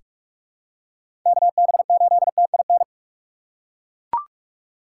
Morse code, heard as a keyed tone, about 42 wpm, 700 Hz. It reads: KB8TIN
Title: USA Call Signs - 6 Characters - 3x Character Spacing - 45wpm